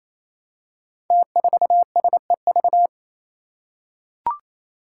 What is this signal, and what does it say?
Morse code, heard as a keyed tone, about 28 wpm, 700 Hz. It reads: T4SEV
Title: Five Challenging Letter-Number Combinations 28wpm